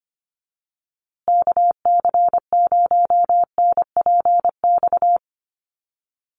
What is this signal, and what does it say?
Morse code, heard as a keyed tone, about 25 wpm, 700 Hz. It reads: KC0NPX
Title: USA Call Signs Repeated 2x 25wpm